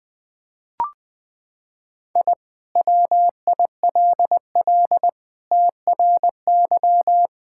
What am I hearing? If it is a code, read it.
Morse code, heard as a keyed tone, about 21 wpm, 700 Hz. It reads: I WILL TRY
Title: Sentences with 3-Word Phrases - 20wpm